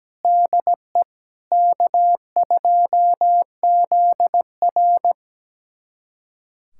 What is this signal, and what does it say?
Morse code, heard as a keyed tone, about 17 wpm, 700 Hz. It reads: DE K2ZR
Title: USA Callsign DE Callsign - 4 Characters - 17wpm